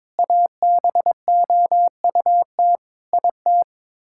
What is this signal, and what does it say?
Morse code, heard as a keyed tone, about 21 wpm, 700 Hz. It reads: ABOUT IT
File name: practice / 22wpm / Sets of 8 Words - 22wpm